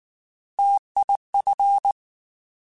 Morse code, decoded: TIF

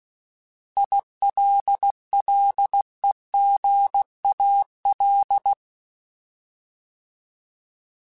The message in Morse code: ILLEGAL

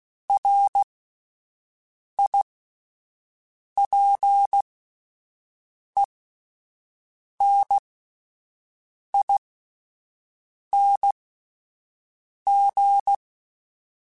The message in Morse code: RIPENING

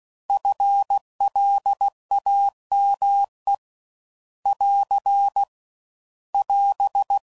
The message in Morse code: FLAME <AR> <AS>

